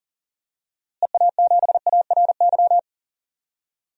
Morse code, heard as a keyed tone, about 40 wpm, 700 Hz. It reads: EA7ARY
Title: DXCC-100 Call Signs - 6 Characters - Repeated 3x - 40wpm